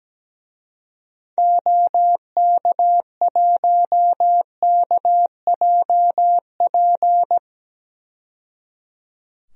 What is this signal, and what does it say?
Morse code, heard as a keyed tone, about 17 wpm, 700 Hz. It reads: OK1KJP
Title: DXCC-100 Call Signs - 6 Characters - 17wpm